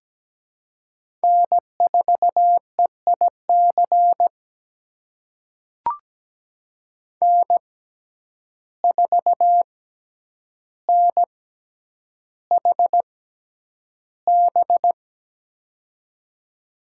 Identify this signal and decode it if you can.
Morse code, heard as a keyed tone, about 17 wpm, spaced as for 9 wpm, 700 Hz. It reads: N4EIC N4NHB
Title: USA Call Signs - 5 Characters - 6x Character Spacing - 17wpm